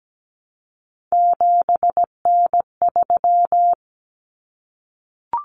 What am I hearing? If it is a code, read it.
Morse code, heard as a keyed tone, about 17 wpm, 690 Hz. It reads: 7N3
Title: Common Cut Numbers - 3-Digit 17wpm